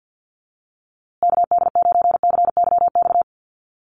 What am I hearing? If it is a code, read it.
Morse code, heard as a keyed tone, about 50 wpm, 700 Hz. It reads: KD9CYX
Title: USA Call Signs Repeated 3x 50wpm